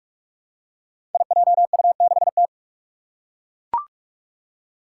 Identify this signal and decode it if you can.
Morse code, heard as a keyed tone, about 42 wpm, 700 Hz. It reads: IJU6T
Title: Five Challenging Letter-Number Combinations 45wpm